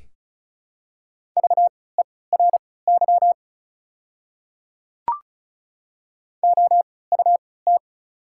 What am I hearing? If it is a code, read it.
Morse code, heard as a keyed tone, about 36 wpm, spaced as for 20 wpm, 700 Hz. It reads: VERY OUT
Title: 3x Character Spacing - Top 200 Words - 35wpm